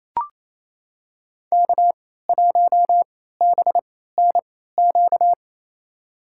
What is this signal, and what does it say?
Morse code, heard as a keyed tone, about 28 wpm, spaced as for 16 wpm, 700 Hz. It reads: K1BNQ
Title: USA Call Signs - 3x Character Spacing - 28wpm